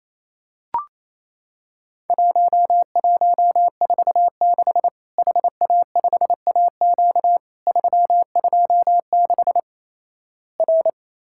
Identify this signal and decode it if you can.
Morse code, heard as a keyed tone, about 28 wpm, 700 Hz. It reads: 1146 HA5AQ 326 R
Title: Worked All Europe DX Contest - QTC Exchange - 28wpm